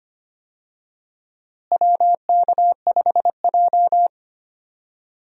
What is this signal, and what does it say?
Morse code, heard as a keyed tone, about 25 wpm, 700 Hz. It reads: WK5J
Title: USA Call Signs - 4 Characters - Repeated 2x - 25wpm